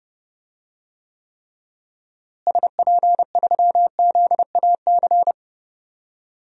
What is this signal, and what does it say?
Morse code, heard as a keyed tone, about 30 wpm, 700 Hz. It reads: SP3ZAC
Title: DXCC-100 Call Signs - 6 Characters - Repeated 3x - 30wpm